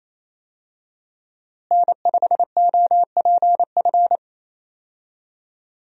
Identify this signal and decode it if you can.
Morse code, heard as a keyed tone, about 27 wpm, 700 Hz. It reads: N5OPF
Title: USA Call Signs - 5 Characters - Repeated 2x - 28wpm